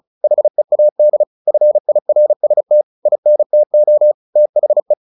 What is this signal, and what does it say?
Morse code, heard as a keyed tone, about 36 wpm, 600 Hz. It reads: HEAD FIRST INTO THE
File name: the old man and the sea / the old man and the sea 1 35wpm